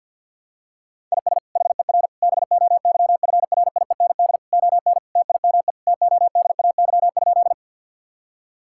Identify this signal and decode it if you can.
Morse code, heard as a keyed tone, about 52 wpm, 700 Hz. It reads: IS HER BOYFRIEND ON TIME TODAY?
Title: Sentences from Top 1000 Words - Review Part 5 - 50wpm